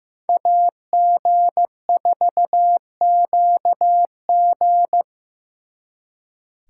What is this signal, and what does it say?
Morse code, heard as a keyed tone, about 18 wpm, 700 Hz. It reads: AG4QG
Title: Speed Racing - USA 2x2 Call Signs 15wpm